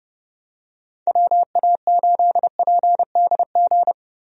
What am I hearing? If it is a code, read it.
Morse code, heard as a keyed tone, about 30 wpm, 700 Hz. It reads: WA8PDG
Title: USA Call Signs - 6 Characters - Repeated 3x - 30wpm